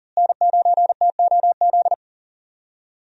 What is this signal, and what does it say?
Morse code, heard as a keyed tone, about 37 wpm, 700 Hz. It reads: N9TOZ
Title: USA Call Signs - 5 Characters - Repeated 3x - 40wpm